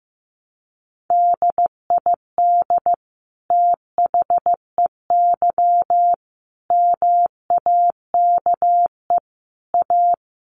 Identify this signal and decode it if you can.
Morse code, heard as a keyed tone, about 15 wpm, 700 Hz. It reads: DID THEY MAKE A
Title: Sentences from Top 800 Words 15wpm